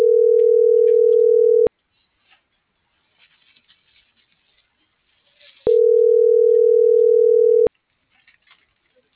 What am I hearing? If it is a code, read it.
Ringback tone (North America), 440 + 480 Hz.